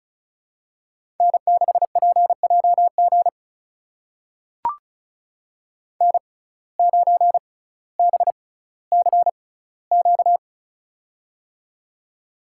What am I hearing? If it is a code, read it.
Morse code, heard as a keyed tone, about 36 wpm, spaced as for 18 wpm, 700 Hz. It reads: N6PJG N9BCQ
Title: USA Call Signs - 5 Characters - 6x Character Spacing - 35wpm